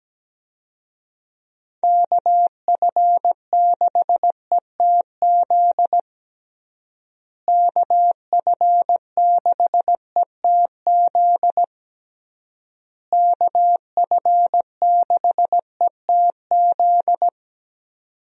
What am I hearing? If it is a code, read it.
Morse code, heard as a keyed tone, about 17 wpm, 700 Hz. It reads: KF6ETZ KF6ETZ KF6ETZ
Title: USA Call Signs - 6 Characters - Repeated 3x - 17wpm